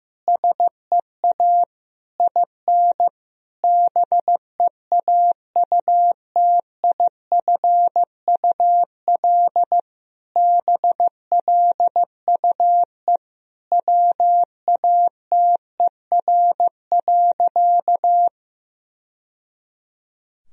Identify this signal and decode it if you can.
Morse code, heard as a keyed tone, about 15 wpm, 700 Hz. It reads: SEA IN BEAUTIFUL BLUE WATER.
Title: Sentences from Top 1000 Words - Review Part 3 - 15wpm